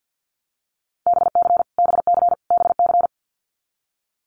Morse code, heard as a keyed tone, about 50 wpm, 700 Hz. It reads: BC BC BC
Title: Canadian Provinces and Territories - Repeated 3x - 50wpm